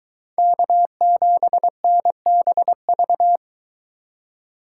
Morse code, heard as a keyed tone, about 22 wpm, 700 Hz. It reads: K7NBV